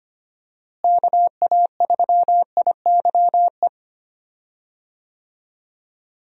Morse code, heard as a keyed tone, about 25 wpm, 700 Hz. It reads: KA3IYE
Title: USA Call Signs - 6 Characters - Repeated 2x - 25wpm